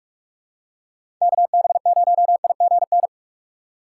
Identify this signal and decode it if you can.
Morse code, heard as a keyed tone, about 46 wpm, 700 Hz. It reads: KB0IGN
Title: USA Call Signs - 6 Characters - 6x Character Spacing - 45wpm